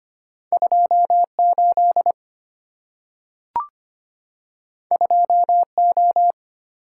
Morse code, heard as a keyed tone, about 25 wpm, 700 Hz. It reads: 28 2O
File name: Two Letter-Number Combinations - 25wpm